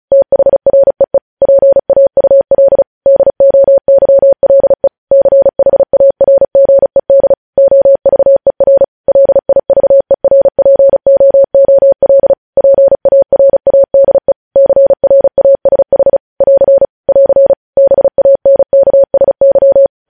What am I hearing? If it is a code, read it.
Morse code, heard as a keyed tone, about 36 wpm, 550 Hz. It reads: THREE PAUL DOYLE CHARGED OVER LIVERPOOL PARADE CRASH <AR> <AR> BANKSY
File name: podcast / News headlines-1x-35x35